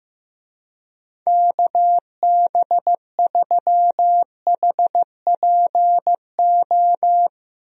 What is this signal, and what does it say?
Morse code, heard as a keyed tone, about 15 wpm, 700 Hz. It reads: KB3HPO
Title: USA Call Signs - 6 Characters - Repeated 3x - 15wpm